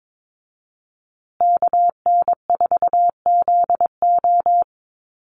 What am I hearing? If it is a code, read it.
Morse code, heard as a keyed tone, about 22 wpm, 700 Hz. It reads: KN4ZO
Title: USA Call Signs Repeated 2x 22wpm